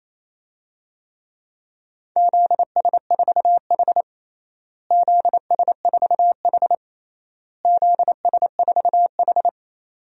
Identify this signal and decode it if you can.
Morse code, heard as a keyed tone, about 28 wpm, 700 Hz. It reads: ZS4H ZS4H ZS4H
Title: DXCC-100 Call Signs - 4 Characters - Repeated 3x - 28wpm